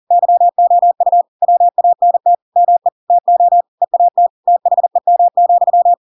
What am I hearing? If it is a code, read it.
Morse code, heard as a keyed tone, about 40 wpm, 700 Hz. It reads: YOU WANT ME TO EAT THEM,
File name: Pinocchio / Pinocchio ch7 40 40 ik1hge